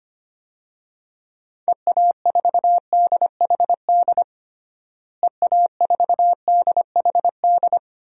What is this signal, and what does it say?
Morse code, heard as a keyed tone, about 25 wpm, 700 Hz. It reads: EA4DHD EA4DHD
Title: DXCC-100 Call Signs - 6 Characters - Repeated 2x - 25wpm